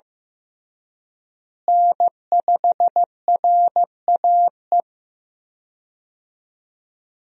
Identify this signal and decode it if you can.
Morse code, heard as a keyed tone, about 15 wpm, 700 Hz. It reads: N5RAE
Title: USA Call Signs - 5 Characters - Repeated 3x - 15wpm